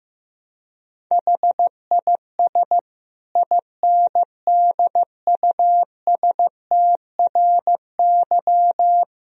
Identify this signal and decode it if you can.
Morse code, heard as a keyed tone, about 15 wpm, 700 Hz. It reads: HIS INDUSTRY
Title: Sentences from Top 1000 Words - Review Part 4 - 15wpm